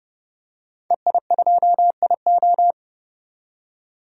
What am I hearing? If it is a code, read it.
Morse code, heard as a keyed tone, about 31 wpm, 700 Hz. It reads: EI2IO
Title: DXCC-100 Call Signs - 5 Characters - 30wpm